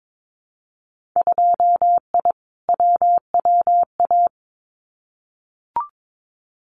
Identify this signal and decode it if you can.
Morse code, heard as a keyed tone, about 22 wpm, 700 Hz. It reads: 2I WWA
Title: Winter Field Day Exchanges Encore 22wpm